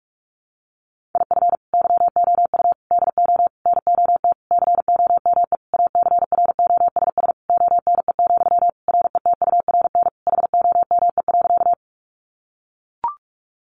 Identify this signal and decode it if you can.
Morse code, heard as a keyed tone, about 44 wpm, 700 Hz. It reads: IF YOU DO NOT COME ACROSS ONE, RETURN HOME.
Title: Sentences from Top 800 Words 45wpm